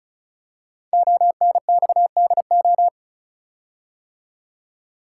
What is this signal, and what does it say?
Morse code, heard as a keyed tone, about 34 wpm, 700 Hz. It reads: ONXDO
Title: Code Groups - 5 Letters - 35wpm